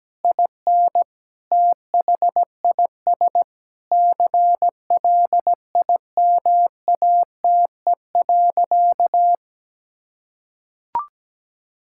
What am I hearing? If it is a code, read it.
Morse code, heard as a keyed tone, about 17 wpm, 700 Hz. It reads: IN THIS CLIMATE.
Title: Sentences from Top 1600 Words 17wpm